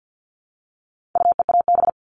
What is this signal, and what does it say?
Morse code, heard as a keyed tone, about 50 wpm, 700 Hz. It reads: UEAB
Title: All Cut Numbers - 4-Digit 50wpm